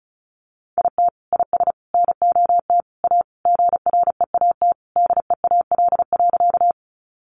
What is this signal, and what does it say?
Morse code, heard as a keyed tone, about 34 wpm, 700 Hz. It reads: IT IS NOT A GREAT DEAL.
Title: Sentences from Top 800 Words 35wpm